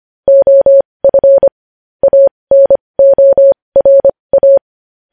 Morse code, heard as a keyed tone, about 25 wpm, spaced as for 20 wpm, 550 Hz. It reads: OF ANORA
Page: News Headlines in Morse Code at 20 WPM